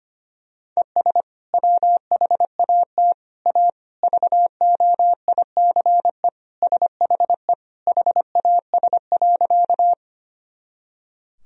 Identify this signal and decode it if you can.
Morse code, heard as a keyed tone, about 25 wpm, 700 Hz. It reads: ES WHAT A VOICE SHE HAS.
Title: Sentences from Top 500 Words and Abbreviations 25wpm